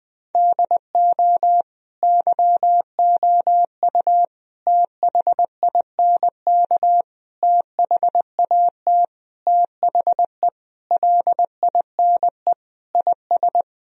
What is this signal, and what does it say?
Morse code, heard as a keyed tone, about 20 wpm, 700 Hz. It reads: DO YOU THINK THAT THE LINE IS
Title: Sentences from Top 1000 Words - Review Part 1 - 20wpm